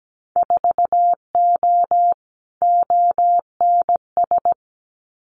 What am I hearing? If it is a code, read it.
Morse code, heard as a keyed tone, about 17 wpm, 700 Hz. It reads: 4O ONS
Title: Winter Field Day Exchanges 17wpm